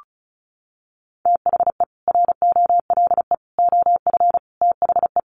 Morse code, heard as a keyed tone, about 34 wpm, 700 Hz. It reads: THE ROLE OF THE
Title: Sets of 4 Words 35wpm